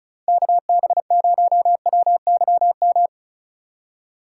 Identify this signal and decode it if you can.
Morse code, heard as a keyed tone, about 44 wpm, 700 Hz. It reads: KB0WYM